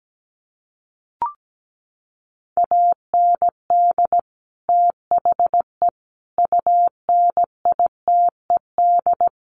Morse code, heard as a keyed tone, about 17 wpm, 700 Hz. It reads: AND THE UNITED
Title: Sets of 4 Words 17wpm